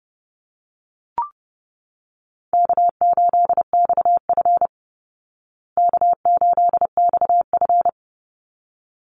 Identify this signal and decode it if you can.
Morse code, heard as a keyed tone, about 30 wpm, 700 Hz. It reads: K8XF K8XF
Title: Easy Call Signs Repeated 2x - 30wpm